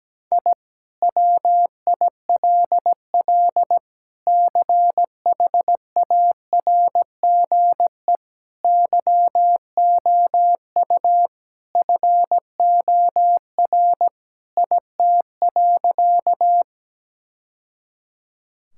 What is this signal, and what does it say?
Morse code, heard as a keyed tone, about 17 wpm, 700 Hz. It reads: I WILL CHARGE YOU FOR IT.
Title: Sentences from Top 1000 Words - Review Part 6 - 17wpm